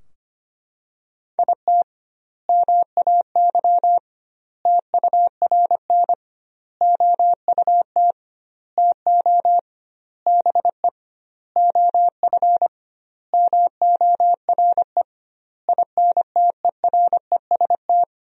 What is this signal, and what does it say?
Morse code, heard as a keyed tone, about 25 wpm, 700 Hz. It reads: IT MAY TURN OUT TO BE OF MORE INTEREST
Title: 2x Word-Spacing - Sentences from Top 400 Words - 25wpm